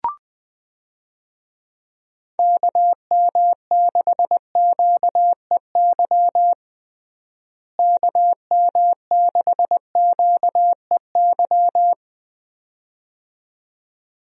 Morse code, heard as a keyed tone, about 20 wpm, 700 Hz. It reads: KM6QEY KM6QEY
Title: USA Call Signs - 6 Characters - Repeated 2x - 20wpm